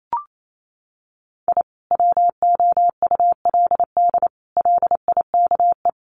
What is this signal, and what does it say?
Morse code, heard as a keyed tone, about 28 wpm, 700 Hz. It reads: I WOULD LIKE
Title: Sets of 5 Words Encore2 28wpm